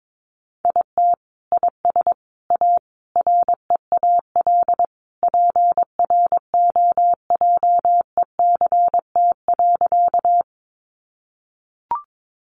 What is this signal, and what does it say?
Morse code, heard as a keyed tone, about 22 wpm, 700 Hz. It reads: IT IS A REAL PROJECT.
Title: Sentences from Top 1400 Words 22wpm